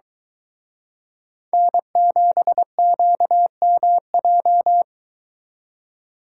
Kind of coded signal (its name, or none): Morse code